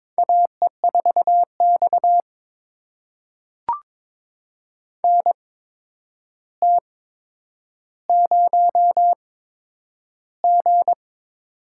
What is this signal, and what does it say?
Morse code, heard as a keyed tone, about 22 wpm, spaced as for 9 wpm, 700 Hz. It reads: AE4X NT0G
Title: USA Call Signs - 4 Characters - 8x Character Spacing - 22wpm